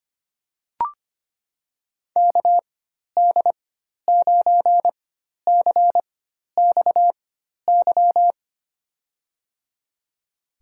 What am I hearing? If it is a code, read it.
Morse code, heard as a keyed tone, about 25 wpm, spaced as for 12 wpm, 700 Hz. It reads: KD9CXY
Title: USA Call Signs - 6 Characters - 4x Character Spacing - 25wpm